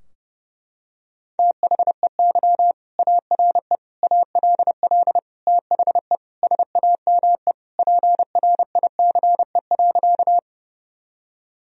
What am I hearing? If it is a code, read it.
Morse code, heard as a keyed tone, about 30 wpm, 700 Hz. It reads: THEY ARE ALL THE SAME PRICE.
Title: Sentences from Top 1000 Words - Review Part 4 - 30wpm